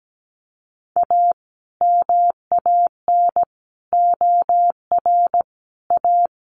Morse code, heard as a keyed tone, about 17 wpm, 700 Hz. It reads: A MAN OR A
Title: Sentences from Top 1100 Words 17wpm